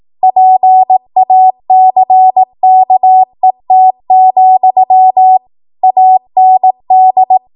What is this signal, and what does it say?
Morse code, heard as a keyed tone, about 18 wpm, 750 Hz. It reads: PACKET, AND
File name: Morse / 18WPM